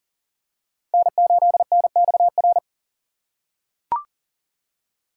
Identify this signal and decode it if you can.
Morse code, heard as a keyed tone, about 40 wpm, 700 Hz. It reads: N8NXR
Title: USA Call Signs - 5 Characters - 4x Character Spacing - 40wpm